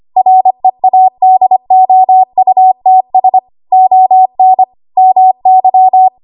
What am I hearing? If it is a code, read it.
Morse code, heard as a keyed tone, about 25 wpm, 750 Hz. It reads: READOUTS ON MY
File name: Archive / 25 WPM / 25WPM